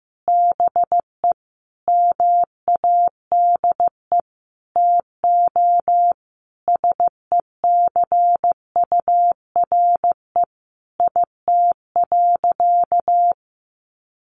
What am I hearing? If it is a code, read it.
Morse code, heard as a keyed tone, about 14 wpm, 700 Hz. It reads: BE MADE TO SECURE IT.
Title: Sentences from Top 1200 Words 15wpm